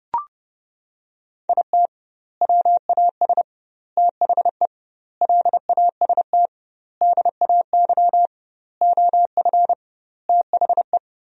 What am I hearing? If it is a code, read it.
Morse code, heard as a keyed tone, about 32 wpm, 700 Hz. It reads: IT WAS THE LAST DAY OF THE
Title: 2x Word-Spacing - Sentences from Top 300 Words - 30wpm